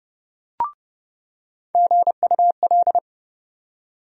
Morse code, heard as a keyed tone, about 32 wpm, 700 Hz. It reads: GUL
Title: USA Call Sign Postfixes - 3 Letters 30wpm